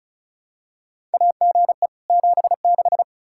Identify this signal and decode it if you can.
Morse code, heard as a keyed tone, about 36 wpm, 700 Hz. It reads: AGE 76
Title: K1USN SST QSO Practice 35wpm